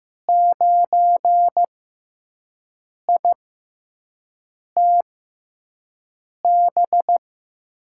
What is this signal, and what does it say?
Morse code, heard as a keyed tone, about 15 wpm, spaced as for 5 wpm, 700 Hz. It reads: 9ITB
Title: USA Call Signs - 5 Characters - 6x Character Spacing - 15wpm